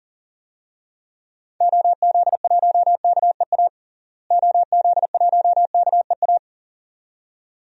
Morse code, heard as a keyed tone, about 37 wpm, 700 Hz. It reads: OZ1KEA OZ1KEA
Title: DXCC-100 Call Signs - 6 Characters - Repeated 2x - 40wpm